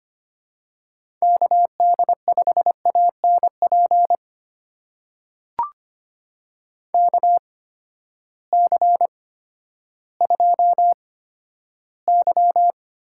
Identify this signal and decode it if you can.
Morse code, heard as a keyed tone, about 25 wpm, spaced as for 12 wpm, 700 Hz. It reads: KD5ANP KC2Y
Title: USA Call Signs - 6 Characters - 8x Character Spacing - 25wpm